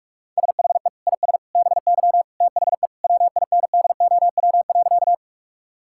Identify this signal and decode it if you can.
Morse code, heard as a keyed tone, about 45 wpm, 700 Hz. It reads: SHE IS BY THE WINDOW.
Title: Sentences from Top 1600 Words 45wpm